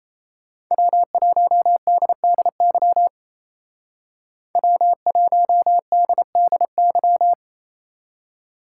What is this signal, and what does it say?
Morse code, heard as a keyed tone, about 33 wpm, 700 Hz. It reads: W1DDY W1DDY
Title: Speed Racing - USA 1x3 Call Signs 28wpm